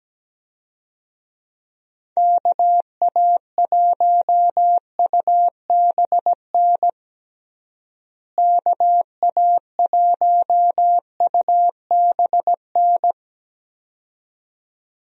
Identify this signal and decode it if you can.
Morse code, heard as a keyed tone, about 17 wpm, 700 Hz. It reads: KA1UBN KA1UBN
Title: USA Call Signs - 6 Characters - Repeated 2x - 17wpm